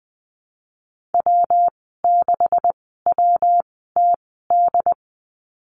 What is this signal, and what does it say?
Morse code, heard as a keyed tone, about 19 wpm, spaced as for 14 wpm, 700 Hz. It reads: W6WTD
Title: 140 Lesson 23g - Letter f - Call Signs 20wpm